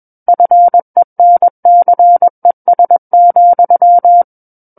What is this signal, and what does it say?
Morse code, heard as a keyed tone, about 21 wpm, 700 Hz. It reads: FENCES,
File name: Book One / 12-Chapter 12 - 21 WPM